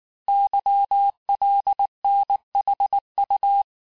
Morse code, heard as a keyed tone, about 19 wpm, 800 Hz. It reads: YLNHU
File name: cours-du-mercredi / audios / cours-869-texte-CHIFFRE-vitesse-18-MPM